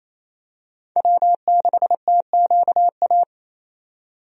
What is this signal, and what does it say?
Morse code, heard as a keyed tone, about 28 wpm, 700 Hz. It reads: W6TQA
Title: USA Call Signs - 5 Characters - 28wpm